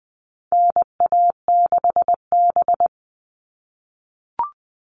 Morse code, heard as a keyed tone, about 20 wpm, 700 Hz. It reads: NA6B
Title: All Cut Numbers - 4-Digit 20wpm